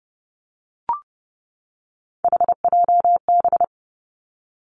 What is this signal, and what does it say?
Morse code, heard as a keyed tone, about 30 wpm, 700 Hz. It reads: HJB
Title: Three Challenging Letter-Number Combinations 30wpm